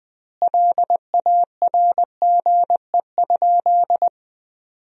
Morse code, heard as a keyed tone, about 20 wpm, 700 Hz. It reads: LARGE?
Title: 2x Word-Spacing - Sentences from Top 200 Words - 20wpm